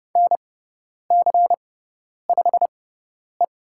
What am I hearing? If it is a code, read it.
Morse code, heard as a keyed tone, about 30 wpm, spaced as for 10 wpm, 690 Hz. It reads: NC5E
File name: Lesson 18g - Letter m - Call Signs 30wpm